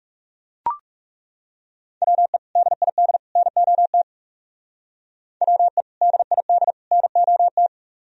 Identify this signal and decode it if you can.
Morse code, heard as a keyed tone, about 39 wpm, 700 Hz. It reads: WE DID NOT WE DID NOT